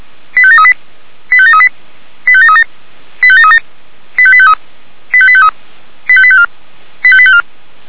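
Five-tone selective call (ZVEI), heard as quick runs of five tones, tones of about 70 ms. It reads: 85738 85738 85737 85737 85853 85853 85854 85854
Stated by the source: ZVEI five-tone selective call received on SDR.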